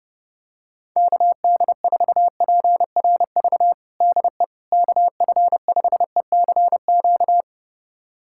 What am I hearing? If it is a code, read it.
Morse code, heard as a keyed tone, about 32 wpm, 700 Hz. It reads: KD4PRV DE KF5ECQ
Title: USA Callsign DE Callsign - 6 Characters - 30wpm